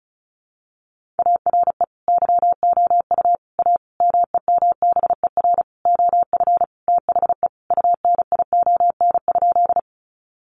Morse code, heard as a keyed tone, about 34 wpm, 700 Hz. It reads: ARE YOU A MEMBER OF THE UNION?
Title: Sentences from Top 1400 Words 35wpm